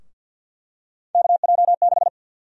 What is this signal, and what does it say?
Morse code, heard as a keyed tone, about 50 wpm, 700 Hz. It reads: KJ6
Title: USA Call Sign Prefixes - Repeated 2x - 50wpm